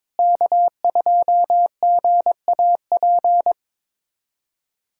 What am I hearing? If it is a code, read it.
Morse code, heard as a keyed tone, about 22 wpm, 700 Hz. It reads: K2GAP
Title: USA Callsign DE Callsign - 5 Characters - 22wpm